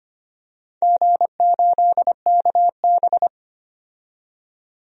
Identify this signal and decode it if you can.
Morse code, heard as a keyed tone, about 25 wpm, 700 Hz. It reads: G8KB
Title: DXCC-100 Call Signs - 4 Characters - 25wpm